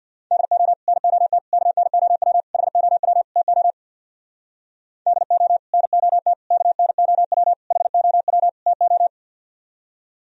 Morse code, heard as a keyed tone, about 62 wpm, 700 Hz. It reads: DO NOT KNOW HOW TO DO NOT KNOW HOW TO
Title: Speed-Racing - Sets of 5 Words 50wpm